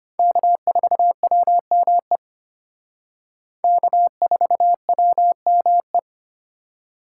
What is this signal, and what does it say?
Morse code, heard as a keyed tone, about 32 wpm, 700 Hz. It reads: K4WME K4WME